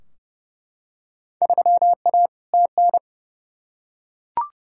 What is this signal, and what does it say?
Morse code, heard as a keyed tone, about 30 wpm, 700 Hz. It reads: 3A TN
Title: Field Day Exchanges - 30wpm